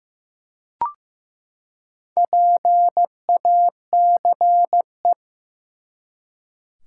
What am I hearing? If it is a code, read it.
Morse code, heard as a keyed tone, about 15 wpm, 700 Hz. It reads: PACE
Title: Four Letter Words - 15wpm